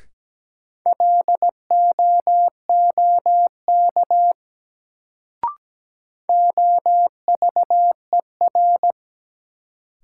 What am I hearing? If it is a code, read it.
Morse code, heard as a keyed tone, about 17 wpm, 700 Hz. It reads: LOOK OVER
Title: Sets of 2 Words - New Word Drill - 17wpm